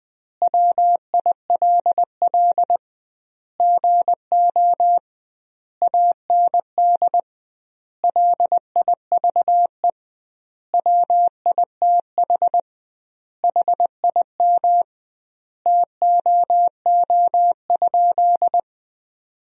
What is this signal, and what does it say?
Morse code, heard as a keyed tone, about 20 wpm, 700 Hz. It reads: WILL GO AND LIVE WITH HIM TOO?
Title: 2x Word-Spacing - Sentences from Top 500 Words - 20wpm